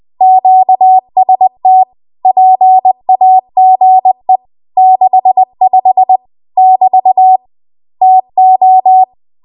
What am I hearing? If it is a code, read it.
Morse code, heard as a keyed tone, about 20 wpm, 750 Hz. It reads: QST PAGE 65 <BT> TO
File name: Morse / 20WPM